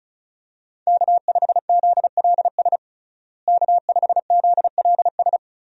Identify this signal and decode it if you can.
Morse code, heard as a keyed tone, about 36 wpm, 700 Hz. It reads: K5ZLS K5ZLS
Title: USA Call Signs - 5 Characters - Repeated 3x - 35wpm